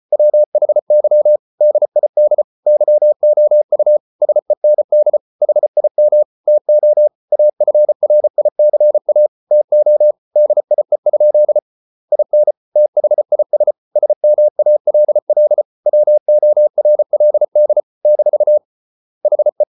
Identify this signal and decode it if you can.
Morse code, heard as a keyed tone, about 34 wpm, 600 Hz. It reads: WHY DID YOU SEND HIM TO AFRICA TO DIE? IN THIS SMALL WORLD - HE